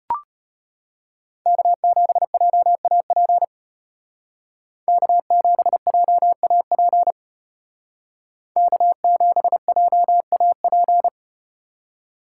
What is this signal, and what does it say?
Morse code, heard as a keyed tone, about 35 wpm, 700 Hz. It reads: K7JAP K7JAP K7JAP